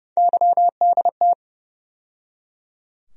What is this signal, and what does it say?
Morse code, heard as a keyed tone, about 31 wpm, 700 Hz. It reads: YDT
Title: USA Call Sign Postfixes - 3 Letters 30wpm